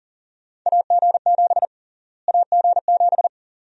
Morse code, heard as a keyed tone, about 40 wpm, 700 Hz. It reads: AG7 AG7
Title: USA Call Sign Prefixes - Repeated 2x - 40wpm